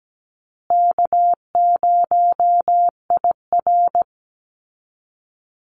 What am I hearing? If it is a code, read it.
Morse code, heard as a keyed tone, about 17 wpm, 700 Hz. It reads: K0IR
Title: Easy Call Signs Repeated 2x - 17wpm